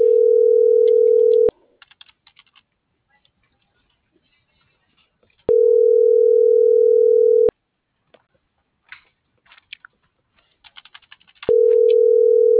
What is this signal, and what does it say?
Ringback tone (North America), 440 + 480 Hz.